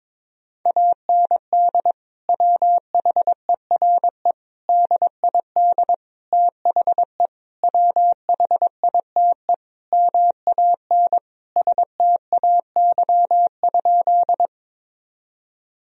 Morse code, heard as a keyed tone, about 21 wpm, 700 Hz. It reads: AND WHERE DID THE WHITE MAN STAY?
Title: Sentences from Top 1000 Words - Review Part 1 - 22wpm